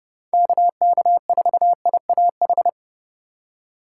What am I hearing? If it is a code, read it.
Morse code, heard as a keyed tone, about 32 wpm, 700 Hz. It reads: KK4IAH
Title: USA Call Signs - 6 Characters - 3x Character Spacing - 30wpm